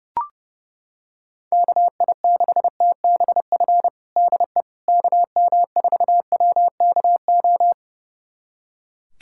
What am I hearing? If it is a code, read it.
Morse code, heard as a keyed tone, about 30 wpm, 700 Hz. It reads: KI6TBF DE KM4WKO
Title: USA Callsign DE Callsign 30wpm